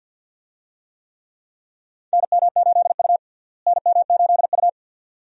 Morse code, heard as a keyed tone, about 47 wpm, 700 Hz. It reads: NM8U NM8U
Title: USA Call Signs - 4 Characters - Repeated 2x - 50wpm